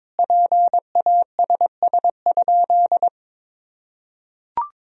Morse code, heard as a keyed tone, about 22 wpm, 700 Hz. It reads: PASS?
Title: Sentences from Top 400 Words and Abbreviations 22wpm